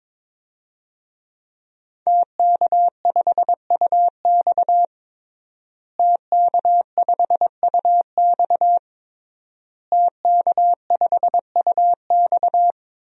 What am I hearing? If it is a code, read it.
Morse code, heard as a keyed tone, about 23 wpm, 700 Hz. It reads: TK5UX TK5UX TK5UX
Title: DXCC-100 Call Signs - 5 Characters - Repeated 3x - 22wpm